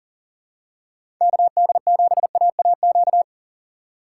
Morse code, heard as a keyed tone, about 37 wpm, 700 Hz. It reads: KD7AAQ
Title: USA Call Signs - 4x Character Spacing - 40wpm